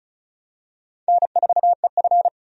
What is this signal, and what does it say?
Morse code, heard as a keyed tone, about 36 wpm, 700 Hz. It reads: N4EF
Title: USA Call Signs - 4 Characters - 35wpm